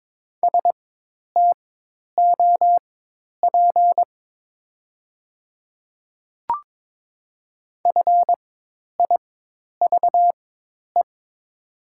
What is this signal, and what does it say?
Morse code, heard as a keyed tone, about 22 wpm, spaced as for 10 wpm, 700 Hz. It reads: STOP FIVE
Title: 4x Character Spacing - Top 500 Words - 22wpm